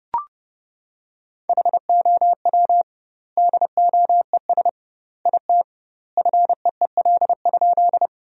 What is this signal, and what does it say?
Morse code, heard as a keyed tone, about 32 wpm, 700 Hz. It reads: HOW DOES IT FEEL?
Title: 2x Word-Spacing - Sentences from Top 400 Words - 30wpm